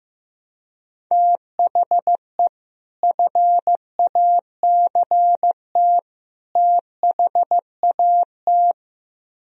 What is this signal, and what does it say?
Morse code, heard as a keyed tone, about 15 wpm, 700 Hz. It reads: THE FACT THAT
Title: Speed-Racing - Sets of 3 Words 15wpm